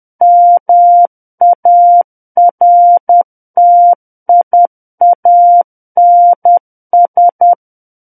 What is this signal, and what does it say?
Morse code, heard as a keyed tone, about 10 wpm, 700 Hz. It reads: MARTIANS
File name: Book One / 01-Chapter 1 - 10 WPM